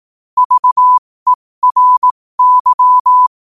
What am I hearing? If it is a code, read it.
Morse code, heard as a keyed tone, about 18 wpm, spaced as for 15 wpm, 1000 Hz. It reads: VERY
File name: qotdfiles / 15wpm